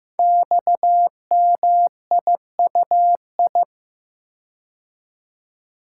Morse code, heard as a keyed tone, about 15 wpm, 700 Hz. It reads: XMIUI
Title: Code Groups - 5 Letters - 15wpm